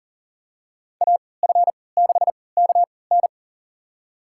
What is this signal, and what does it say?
Morse code, heard as a keyed tone, about 40 wpm, spaced as for 23 wpm, 700 Hz. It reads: AF6KN
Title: USA Call Signs - 5 Characters - 3x Character Spacing - 40wpm